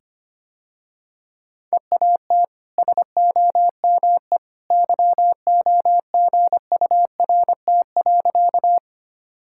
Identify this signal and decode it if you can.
Morse code, heard as a keyed tone, about 25 wpm, 700 Hz. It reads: EAT SOME YOGURT.
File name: Easy Sentences with Hints - 25wpm